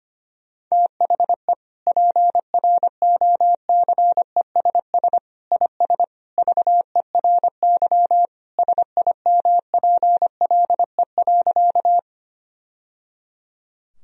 Morse code, heard as a keyed tone, about 25 wpm, 700 Hz. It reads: THE PROCESS IS VERY SIMPLE.
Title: Sentences from Top 1000 Words - Review Part 3 - 25wpm